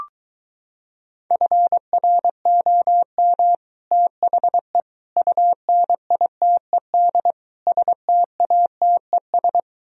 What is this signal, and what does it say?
Morse code, heard as a keyed tone, about 23 wpm, 700 Hz. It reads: FROM THE UNITED STATES